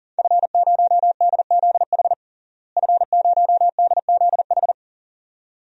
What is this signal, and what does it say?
Morse code, heard as a keyed tone, about 37 wpm, 700 Hz. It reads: F0DZH F0DZH
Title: DXCC-100 Call Signs - 5 Characters - Repeated 2x - 40wpm